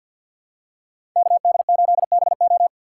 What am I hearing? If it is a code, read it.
Morse code, heard as a keyed tone, about 47 wpm, 700 Hz. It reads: KD8BO
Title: USA Call Signs - 5 Characters - 6x Character Spacing - 50wpm